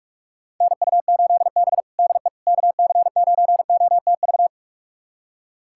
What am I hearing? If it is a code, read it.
Morse code, heard as a keyed tone, about 41 wpm, 700 Hz. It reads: NU8B DE KC9OTV
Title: USA Callsign DE Callsign 45wpm